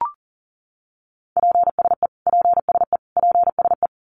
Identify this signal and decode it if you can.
Morse code, heard as a keyed tone, about 40 wpm, 700 Hz. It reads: PSE PSE PSE
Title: Abbreviations Repeated 3x - 40wpm